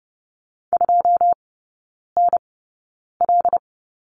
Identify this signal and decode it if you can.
Morse code, heard as a keyed tone, about 30 wpm, spaced as for 9 wpm, 700 Hz. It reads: 2NL
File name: 081 Lesson 14e - Number 2 - Three Characters 30wpm